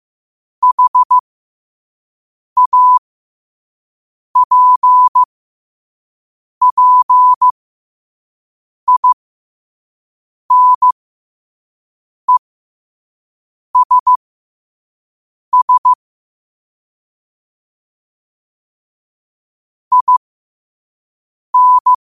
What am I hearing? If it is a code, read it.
Morse code, heard as a keyed tone, about 15 wpm, spaced as for 5 wpm, 1000 Hz. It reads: HAPPINESS IN
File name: qotdfiles / 5wpm